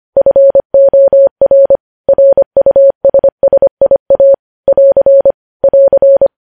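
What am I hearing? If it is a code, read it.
Morse code, heard as a keyed tone, about 25 wpm, 550 Hz. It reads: FOR RUSSIA <AR> <AR>
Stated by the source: Morse code news headlines at 25 wpm.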